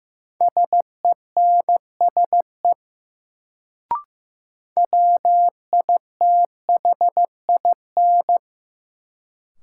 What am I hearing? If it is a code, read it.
Morse code, heard as a keyed tone, about 15 wpm, 700 Hz. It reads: SENSE WITHIN
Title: Sentences from Top 700 Words - New Word Drill - 15wpm